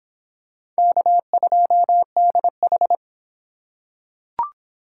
Morse code, heard as a keyed tone, about 25 wpm, 700 Hz. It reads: K2DH